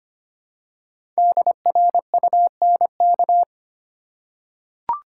Morse code, heard as a keyed tone, about 25 wpm, 700 Hz. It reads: DRUNK
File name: 5 Letter Words - First 3 Letters Given - 25wpm